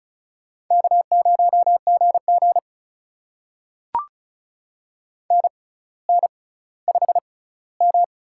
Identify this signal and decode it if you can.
Morse code, heard as a keyed tone, about 34 wpm, spaced as for 18 wpm, 700 Hz. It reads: K0GG NN5M
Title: USA Call Signs - 4 Characters - 6x Character Spacing - 35wpm